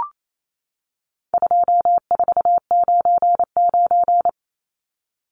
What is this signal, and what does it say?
Morse code, heard as a keyed tone, about 28 wpm, 700 Hz. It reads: 2499
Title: Numbers - 4-Digit 28wpm